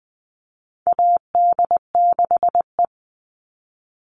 Morse code, heard as a keyed tone, about 20 wpm, 700 Hz. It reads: AD6E
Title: Call Signs Repeated 2x - 20wpm